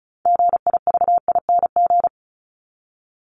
Morse code, heard as a keyed tone, about 34 wpm, 700 Hz. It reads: GIVING
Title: ING Suffix - Top 100 Words with Base Words 35wpm